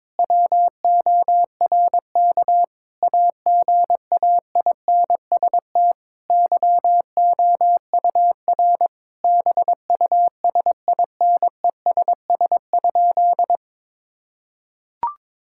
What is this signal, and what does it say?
Morse code, heard as a keyed tone, about 22 wpm, 700 Hz. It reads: WORK AGAINST YOUR BUSINESS?
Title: Sentences from Top 1000 Words - Review Part 5 - 22wpm